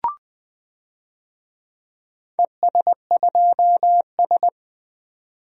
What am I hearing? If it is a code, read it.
Morse code, heard as a keyed tone, about 20 wpm, 700 Hz. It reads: ES2S
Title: DXCC-100 Call Signs - 4 Characters - 20wpm